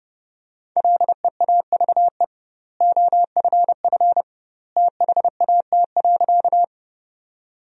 Morse code, heard as a keyed tone, about 30 wpm, 700 Hz. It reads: LEAVE OFF THAT.
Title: 2x Word-Spacing - Sentences from Top 400 Words - 30wpm